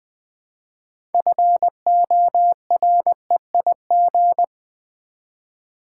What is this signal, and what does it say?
Morse code, heard as a keyed tone, about 20 wpm, 700 Hz. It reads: FOREIG
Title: Build A Word - 3 Letters At A Time - 9-Letter Words - 20wpm